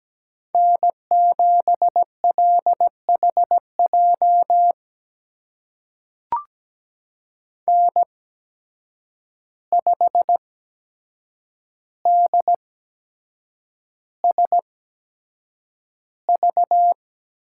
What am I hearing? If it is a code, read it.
Morse code, heard as a keyed tone, about 17 wpm, spaced as for 7 wpm, 700 Hz. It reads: N7LHJ N5DSV
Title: USA Call Signs - 5 Characters - 8x Character Spacing - 17wpm